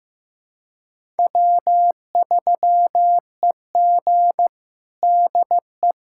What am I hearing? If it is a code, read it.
Morse code, heard as a keyed tone, about 15 wpm, 700 Hz. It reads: W3EG DE
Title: USA Callsign DE Callsign - 4 Characters - 15wpm